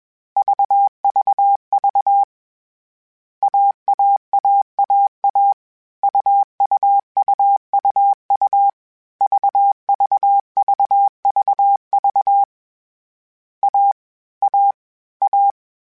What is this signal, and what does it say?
Morse code, heard as a keyed tone, about 21 wpm, 800 Hz. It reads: VVV AAAAA UUUUU VVVVV A A A